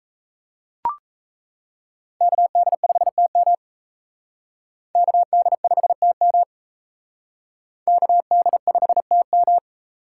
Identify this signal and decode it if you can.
Morse code, heard as a keyed tone, about 36 wpm, 700 Hz. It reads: KD5TM KD5TM KD5TM